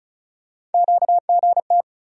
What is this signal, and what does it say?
Morse code, heard as a keyed tone, about 36 wpm, 700 Hz. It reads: QGT
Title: USA Call Sign Postfixes - 3 Letters 35wpm